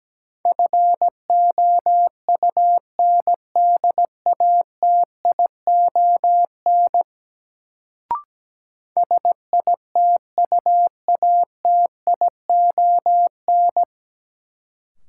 Morse code, heard as a keyed tone, about 17 wpm, 700 Hz. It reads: FOUNDATION SITUATION
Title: Sentences from Top 1200 Words - New Word Drill - 17wpm